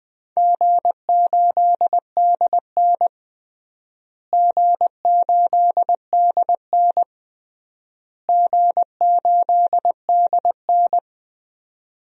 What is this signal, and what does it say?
Morse code, heard as a keyed tone, about 21 wpm, 700 Hz. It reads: G8DN G8DN G8DN
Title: DXCC-100 Call Signs - 4 Characters - Repeated 3x - 20wpm